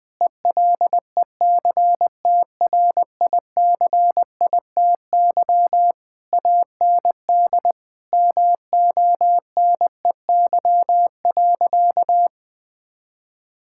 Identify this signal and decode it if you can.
Morse code, heard as a keyed tone, about 20 wpm, 700 Hz. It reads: ELECTRICITY AND MONEY.
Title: Difficult Sentences with Hints - 20wpm